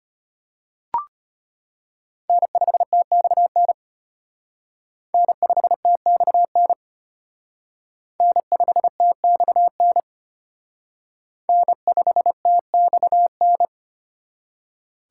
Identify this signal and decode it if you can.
Morse code, heard as a keyed tone, about 32 wpm, 700 Hz. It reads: N5TXN N5TXN N5TXN N5TXN